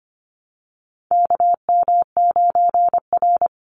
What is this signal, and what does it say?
Morse code, heard as a keyed tone, about 25 wpm, 700 Hz. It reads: KM9R
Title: Call Signs - 25wpm